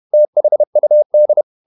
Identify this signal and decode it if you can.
Morse code, heard as a keyed tone, about 31 wpm, 600 Hz. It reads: THUD
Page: Never 31wpm | CW med Gnister